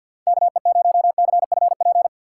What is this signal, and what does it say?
Morse code, heard as a keyed tone, about 47 wpm, 700 Hz. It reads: KE0CFP
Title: USA Call Signs - 6 Characters - 4x Character Spacing - 50wpm